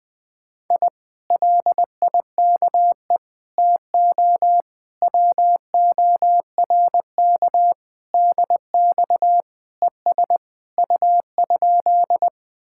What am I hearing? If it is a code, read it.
Morse code, heard as a keyed tone, about 20 wpm, 700 Hz. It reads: I LIKE TO WORK DX ES U?
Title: Speed-Racing Common QSO Phrases - 17wpm